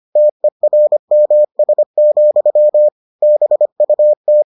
Morse code, heard as a keyed tone, about 25 wpm, 600 Hz. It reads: TERMS, BUT